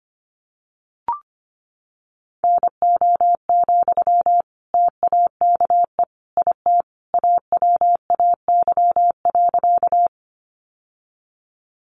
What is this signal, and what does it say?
Morse code, heard as a keyed tone, about 25 wpm, 700 Hz. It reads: NO, TAKE IT AWAY.
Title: Sentences from Top 600 Words 25wpm